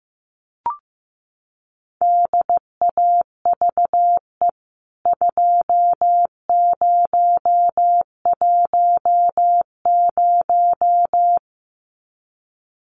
Morse code, heard as a keyed tone, about 15 wpm, 700 Hz. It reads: DAVE 2010
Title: CWT Exchange 15wpm